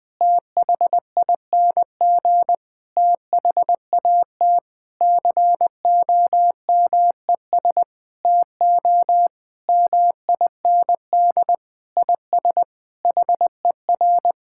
Morse code, heard as a keyed tone, about 20 wpm, 700 Hz. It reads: THING THAT COMES TO MIND IS HER
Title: Sentences with 8-Word Phrases - 20wpm